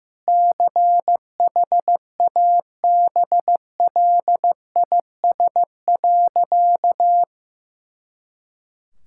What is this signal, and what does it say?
Morse code, heard as a keyed tone, about 15 wpm, 700 Hz. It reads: CHABLIS.
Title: Pangram Challenge v2 - 15wpm